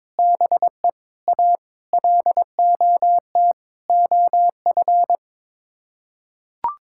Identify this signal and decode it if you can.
Morse code, heard as a keyed tone, about 22 wpm, 700 Hz. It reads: BE A LOT OF
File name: practice / 22wpm / Sets of 7 Words - 22wpm